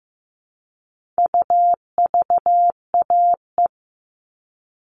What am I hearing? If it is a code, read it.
Morse code, heard as a keyed tone, about 15 wpm, 700 Hz. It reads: UVAE
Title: All Cut Numbers - 4-Digit 15wpm